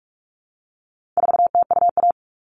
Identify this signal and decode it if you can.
Morse code, heard as a keyed tone, about 44 wpm, 700 Hz. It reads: 4TUA
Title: All Cut Numbers - 4-Digit 45wpm